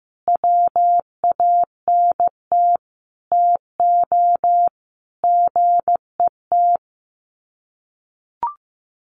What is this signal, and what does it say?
Morse code, heard as a keyed tone, about 15 wpm, 700 Hz. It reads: WANT TO GET
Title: Sets of 5 Words Encore 15wpm